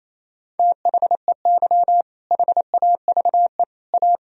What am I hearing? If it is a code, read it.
Morse code, heard as a keyed tone, about 27 wpm, 700 Hz. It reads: THEY HAVE A
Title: Sentences from Top 1000 Words - Review Part 4 - 28wpm